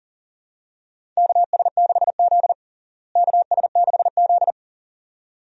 Morse code, heard as a keyed tone, about 40 wpm, 700 Hz. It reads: KS6Z KS6Z
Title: USA Call Signs - 4 Characters - Repeated 2x - 40wpm